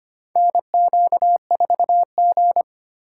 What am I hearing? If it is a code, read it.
Morse code, heard as a keyed tone, about 25 wpm, 700 Hz. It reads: NQ4G
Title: USA Call Signs - 4 Characters - 4x Character Spacing - 25wpm